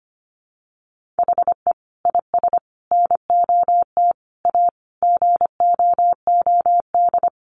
Morse code, heard as a keyed tone, about 25 wpm, 700 Hz. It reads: HE IS NOT A GOOD
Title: Sentences from Top 1400 Words 25wpm